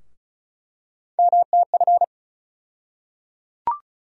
Morse code, heard as a keyed tone, about 36 wpm, 700 Hz. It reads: MTF
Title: USA Call Sign Postfixes - 3 Letters 35wpm